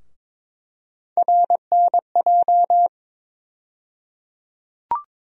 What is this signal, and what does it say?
Morse code, heard as a keyed tone, about 22 wpm, 700 Hz. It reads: RNJ
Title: USA Call Sign Postfixes - 3 Letters 22wpm